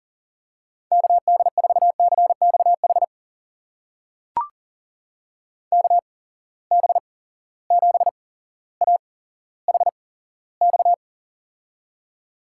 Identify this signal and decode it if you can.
Morse code, heard as a keyed tone, about 40 wpm, spaced as for 17 wpm, 700 Hz. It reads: KD4CXH KB7AHX
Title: USA Call Signs - 6 Characters - 8x Character Spacing - 40wpm